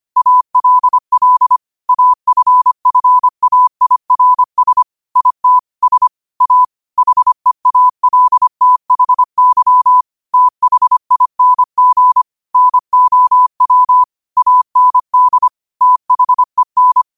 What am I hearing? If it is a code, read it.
Morse code, heard as a keyed tone, about 25 wpm, 1000 Hz. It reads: ALL AFFAIRS IT S A HEALTHY THING NOW AND THEN
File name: qotdfiles / 25wpm